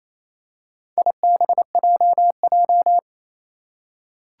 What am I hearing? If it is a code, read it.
Morse code, heard as a keyed tone, about 28 wpm, 700 Hz. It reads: IBJJ
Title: Four Challenging Letter-Number Combinations 28wpm